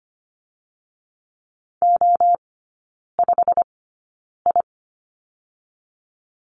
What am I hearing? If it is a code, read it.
Morse code, heard as a keyed tone, about 25 wpm, spaced as for 9 wpm, 700 Hz. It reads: O5I
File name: Lesson 15e - Number 5 - Three Characters 25wpm